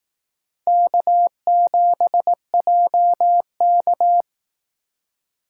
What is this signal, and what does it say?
Morse code, heard as a keyed tone, about 18 wpm, 700 Hz. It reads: K7JK
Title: Speed Racing - USA 1x2 Call Signs 15wpm